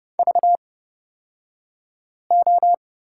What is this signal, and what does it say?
Morse code, heard as a keyed tone, about 29 wpm, spaced as for 5 wpm, 700 Hz. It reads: VO
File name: Lesson 36d - BT - Two Characters 30wpm